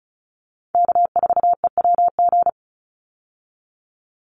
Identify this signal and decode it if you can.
Morse code, heard as a keyed tone, about 34 wpm, 700 Hz. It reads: K4EWG
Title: Call Signs Repeated 3x - 35wpm